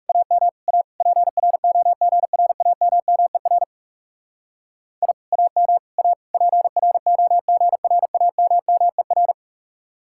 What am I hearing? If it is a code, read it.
Morse code, heard as a keyed tone, about 37 wpm, 700 Hz. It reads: AM A PROGRAMMER I AM A PROGRAMMER
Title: Speed-Racing Common QSO Phrases - 30wpm